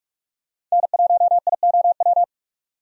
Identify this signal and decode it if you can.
Morse code, heard as a keyed tone, about 43 wpm, 700 Hz. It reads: N1IOW
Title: USA Call Signs - 5 Characters - 45wpm